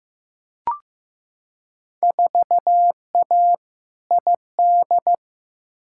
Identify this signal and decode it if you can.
Morse code, heard as a keyed tone, about 15 wpm, 700 Hz. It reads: 4A ID
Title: Field Day Exchanges - 15wpm